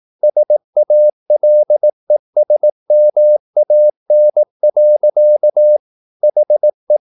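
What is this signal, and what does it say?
Morse code, heard as a keyed tone, about 18 wpm, 600 Hz. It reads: SALESMAN. HE
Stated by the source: Morse practice at 18 wpm, English text, from CW med Gnister.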